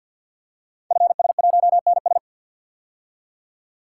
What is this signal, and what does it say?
Morse code, heard as a keyed tone, about 46 wpm, 700 Hz. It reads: FS1NS
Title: Code Groups - 5 Letter:Numbers - 50wpm